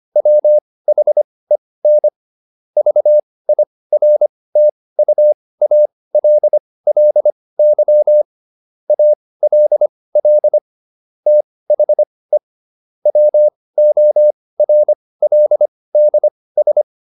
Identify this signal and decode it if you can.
Morse code, heard as a keyed tone, about 25 wpm, spaced as for 18 wpm, 600 Hz. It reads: WHEN VIRTUALLY ALL THE WORLDS